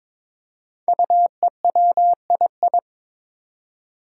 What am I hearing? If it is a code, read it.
Morse code, heard as a keyed tone, about 22 wpm, 700 Hz. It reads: UEWII
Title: Five Challenging Letter-Number Combinations 22wpm